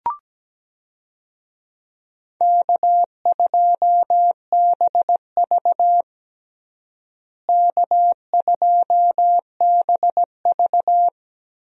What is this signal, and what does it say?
Morse code, heard as a keyed tone, about 17 wpm, 700 Hz. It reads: K2BV K2BV
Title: USA Call Signs - 4 Characters - Repeated 2x - 17wpm